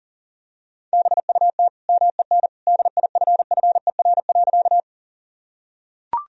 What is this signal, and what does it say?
Morse code, heard as a keyed tone, about 37 wpm, 700 Hz. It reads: BUT MEN DIFFER.
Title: Sentences from Top 1000 Words - Review Part 1 - 40wpm